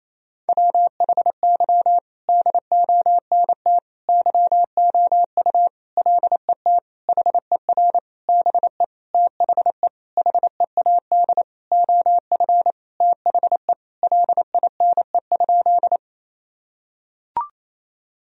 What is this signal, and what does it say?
Morse code, heard as a keyed tone, about 28 wpm, 700 Hz. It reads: WHY DONT YOU LET HER BE THE HEAD OF THE LINE?
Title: Sentences from Top 1000 Words - Review Part 1 - 28wpm